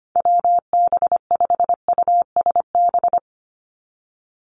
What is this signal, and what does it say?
Morse code, heard as a keyed tone, about 25 wpm, 700 Hz. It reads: WB5USB
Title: NA SOTA Chaser Callsigns 25wpm